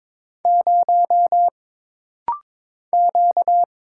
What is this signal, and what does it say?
Morse code, heard as a keyed tone, about 22 wpm, 700 Hz. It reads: 0 Q
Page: This rapid-fire timing is geared to push your ICR (Instant Character Recognition) skills.